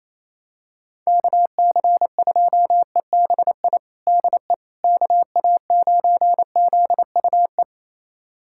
Morse code, heard as a keyed tone, about 28 wpm, 700 Hz. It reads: KC2EBI DE KA9ZUE
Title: USA Callsign DE Callsign 28wpm